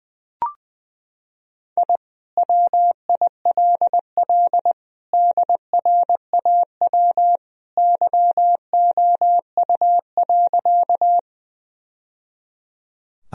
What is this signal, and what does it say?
Morse code, heard as a keyed tone, about 20 wpm, 700 Hz. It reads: I WILL DRAW YOU.
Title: Sentences from Top 1000 Words - Review Part 1 - 20wpm